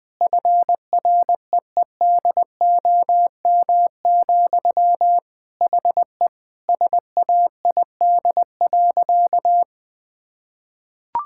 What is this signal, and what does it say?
Morse code, heard as a keyed tone, about 20 wpm, 700 Hz. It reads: FREEDOM, HE SAID.
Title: Sentences from Top 1000 Words - Review Part 5 - 20wpm